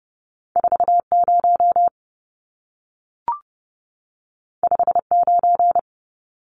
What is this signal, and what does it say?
Morse code, heard as a keyed tone, about 30 wpm, 700 Hz. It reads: 40 59
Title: Numbers - 2-Digit 30wpm